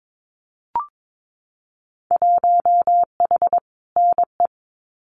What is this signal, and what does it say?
Morse code, heard as a keyed tone, about 22 wpm, 700 Hz. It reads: 1H NE
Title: Winter Field Day Exchanges 22wpm